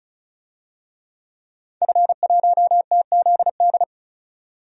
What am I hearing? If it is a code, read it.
Morse code, heard as a keyed tone, about 36 wpm, 700 Hz. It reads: F1TZD
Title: DXCC-100 Call Signs - 5 Characters - 35wpm